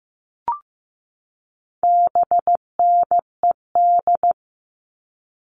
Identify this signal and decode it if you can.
Morse code, heard as a keyed tone, about 15 wpm, 700 Hz. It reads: BNED
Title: All Cut Numbers - 4-Digit 15wpm